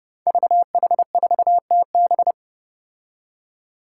Morse code, heard as a keyed tone, about 30 wpm, 700 Hz. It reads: VH4TB
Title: Five Challenging Letter-Number Combinations 30wpm